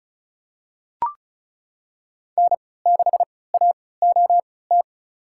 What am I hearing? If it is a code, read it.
Morse code, heard as a keyed tone, about 36 wpm, spaced as for 20 wpm, 700 Hz. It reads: N6AOT
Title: USA Call Signs - 5 Characters - 3x Character Spacing - 35wpm